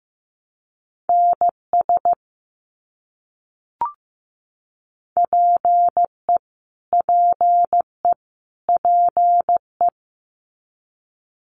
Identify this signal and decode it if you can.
Morse code, heard as a keyed tone, about 15 wpm, 700 Hz. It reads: NS PE PE PE
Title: Canadian Provinces and Territories - Repeated 3x - 15wpm